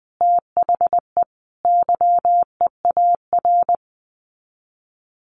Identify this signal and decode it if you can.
Morse code, heard as a keyed tone, about 20 wpm, 700 Hz. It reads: THE YEAR
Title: Sets of 5 Words Encore 20wpm